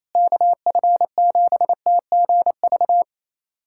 Morse code, heard as a keyed tone, about 28 wpm, 700 Hz. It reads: KF7TGV
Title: USA Callsign DE Callsign - 6 Characters - 28wpm